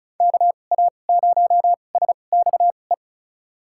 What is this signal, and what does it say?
Morse code, heard as a keyed tone, about 36 wpm, spaced as for 25 wpm, 700 Hz. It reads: KA0SXE
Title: USA Call Signs - 6 Characters - 2x Character Spacing - 35wpm